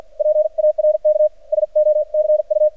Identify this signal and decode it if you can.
Morse code, heard as a keyed tone, about 48 wpm, 610 Hz. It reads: PARK SO CU
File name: archived / CRSnet